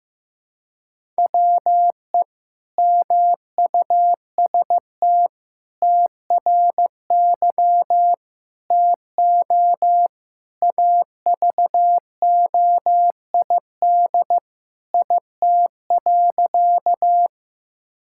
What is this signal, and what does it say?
Morse code, heard as a keyed tone, about 16 wpm, 700 Hz. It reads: WE MUST TRY TO AVOID IT.
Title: Sentences from Top 1000 Words - Review Part 6 - 15wpm